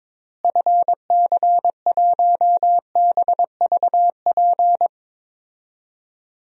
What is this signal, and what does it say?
Morse code, heard as a keyed tone, about 22 wpm, 700 Hz. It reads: FC1BVP
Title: DXCC-100 Call Signs - 6 Characters - Repeated 3x - 22wpm